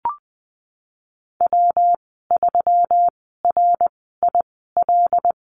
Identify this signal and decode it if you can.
Morse code, heard as a keyed tone, about 20 wpm, spaced as for 15 wpm, 700 Hz. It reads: W3RIL
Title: 122 Lesson 20g - Number 3 - Call Signs 20wpm